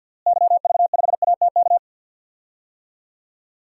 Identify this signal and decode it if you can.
Morse code, heard as a keyed tone, about 50 wpm, 700 Hz. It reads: YV5ATK
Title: DXCC-100 Call Signs - 6 Characters - Repeated 3x - 50wpm